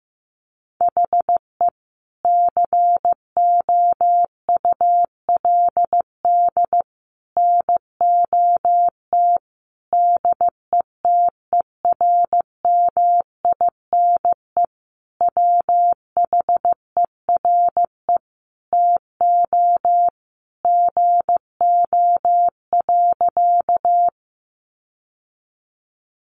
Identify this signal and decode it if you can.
Morse code, heard as a keyed tone, about 15 wpm, 700 Hz. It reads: HE COULD NOT DETERMINE WHERE TO GO.
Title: Sentences from Top 700 Words 15wpm